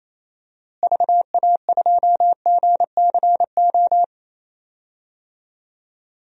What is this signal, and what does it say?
Morse code, heard as a keyed tone, about 27 wpm, 700 Hz. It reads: VA2GCO
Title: DXCC-100 Call Signs - 6 Characters - Repeated 2x - 28wpm